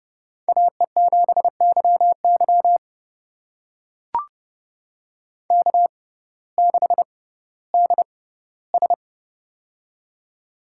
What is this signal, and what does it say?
Morse code, heard as a keyed tone, about 30 wpm, spaced as for 17 wpm, 700 Hz. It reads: AE7YY K6DS
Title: USA Call Signs - 6x Character Spacing - 30wpm